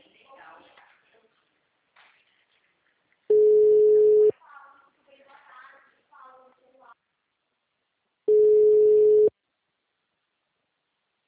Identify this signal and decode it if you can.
Ringback tone (Europe, 425 Hz), with speech.
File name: out- -2002- - - .33322